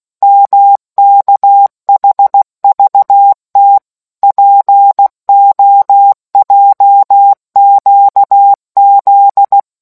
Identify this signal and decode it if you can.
Morse code, heard as a keyed tone, about 16 wpm, 800 Hz. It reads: MKHVT POJQZ